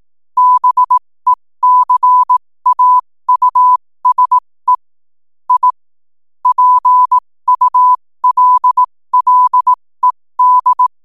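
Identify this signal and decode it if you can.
Morse code, heard as a keyed tone, about 18 wpm, spaced as for 15 wpm, 1000 Hz. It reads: BECAUSE I PULLED
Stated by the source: 15 wpm Morse code from RNZN Communicators Association.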